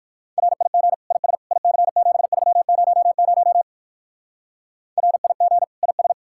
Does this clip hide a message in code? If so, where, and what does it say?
Yes: Morse code from 0.4 to 6.1 s, heard as a keyed tone, about 56 wpm, 700 Hz. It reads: RIG IS IC7300 RIG IS